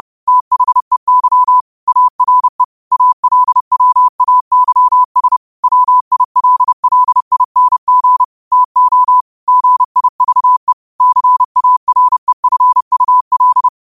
Morse code, heard as a keyed tone, about 30 wpm, 1000 Hz. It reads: THEY ARE ALWAYS WILLING TO GIVE CAREFUL
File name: qotdfiles / 30wpm